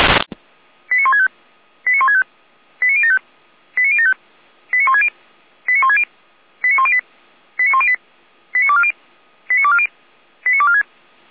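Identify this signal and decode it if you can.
Five-tone selective call (ZVEI), heard as quick runs of five tones, tones of about 80 ms. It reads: 89165 89165 89075 89075 89160 89160 89108 89108 89240 89240 89246
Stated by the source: Five-tone ZVEI selective call received by SDR.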